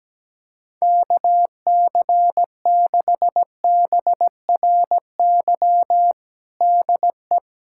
Morse code, heard as a keyed tone, about 17 wpm, 700 Hz. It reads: KC6BRY DE
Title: USA Callsign DE Callsign - 6 Characters - 17wpm